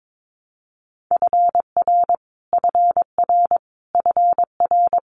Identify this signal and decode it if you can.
Morse code, heard as a keyed tone, about 22 wpm, 700 Hz. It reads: FR FR FR
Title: Multi-letter Phonemes Repeated 3x 22wpm